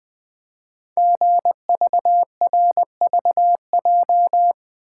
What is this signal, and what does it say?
Morse code, heard as a keyed tone, about 20 wpm, 700 Hz. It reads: GVRVJ
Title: Code Groups - 5 Letter:Numbers - 20wpm